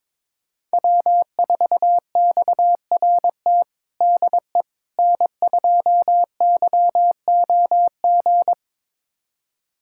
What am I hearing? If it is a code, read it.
Morse code, heard as a keyed tone, about 22 wpm, 700 Hz. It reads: W4XRT DE N2YOG
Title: USA Callsign DE Callsign 22wpm